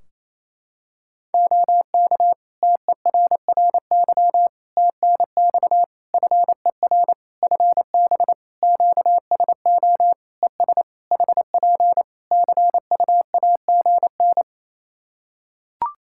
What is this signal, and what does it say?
Morse code, heard as a keyed tone, about 27 wpm, 700 Hz. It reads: OK TERRY TNX FER FB QSO ES HP CUAGN
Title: Practice QSO - Exchange 4 - Ending - 28wpm